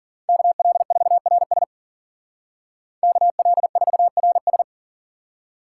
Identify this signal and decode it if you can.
Morse code, heard as a keyed tone, about 48 wpm, 700 Hz. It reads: KL4RS KL4RS
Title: Speed Racing - USA 2x2 Call Signs 40wpm